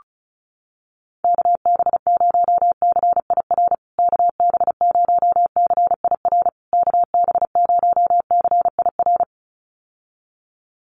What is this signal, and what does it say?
Morse code, heard as a keyed tone, about 34 wpm, 700 Hz. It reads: KB0CIR KB0CIR KB0CIR
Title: USA Call Signs Repeated 3x 35wpm